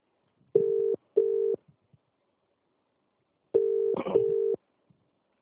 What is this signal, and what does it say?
Ringback tone (UK), 400 + 450 Hz.